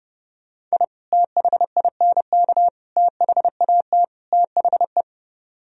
Morse code, heard as a keyed tone, about 40 wpm, 700 Hz. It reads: I THINK THAT THE